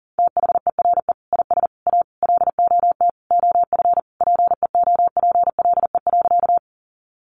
Morse code, heard as a keyed tone, about 40 wpm, 700 Hz. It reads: THERE IS A LOT OF PEOPLE.
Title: Sentences from Top 500 Words 40wpm